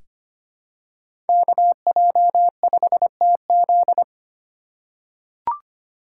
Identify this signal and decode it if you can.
Morse code, heard as a keyed tone, about 25 wpm, 700 Hz. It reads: KJ5TZ
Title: USA Call Signs - 5 Characters - 4x Character Spacing - 25wpm